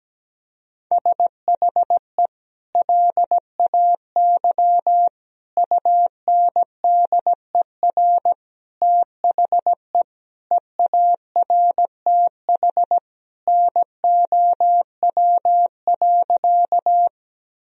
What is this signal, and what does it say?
Morse code, heard as a keyed tone, about 17 wpm, 700 Hz. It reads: SHE LAY UNDER THE EARTH NOW.
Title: Sentences from Top 1000 Words - Review Part 2 - 17wpm